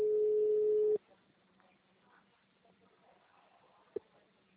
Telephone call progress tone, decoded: ringback tone

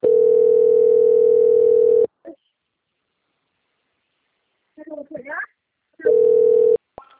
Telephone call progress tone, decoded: ringback tone

